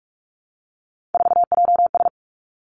Morse code, heard as a keyed tone, about 46 wpm, 700 Hz. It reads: 4JS
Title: Three Challenging Letter-Number Combinations 45wpm